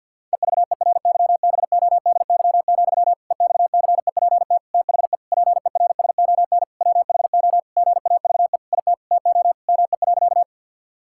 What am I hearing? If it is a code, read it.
Morse code, heard as a keyed tone, about 50 wpm, 700 Hz. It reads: EVERYBODY, EXCEPT THE PERSON WHO GAVE IT TO ME.
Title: Sentences from Top 1600 Words 50wpm